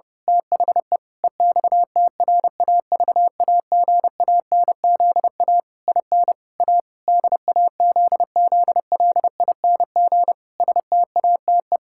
Morse code, heard as a keyed tone, about 30 wpm, 700 Hz. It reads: THE EXTRAVAGANZA IN A DAZZLING STATE
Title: Pangram Challenge v2 - 30wpm